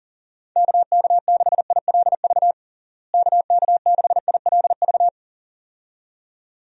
Morse code, heard as a keyed tone, about 40 wpm, 700 Hz. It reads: KK6ILV KK6ILV
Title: USA Call Signs - 6 Characters - Repeated 2x - 40wpm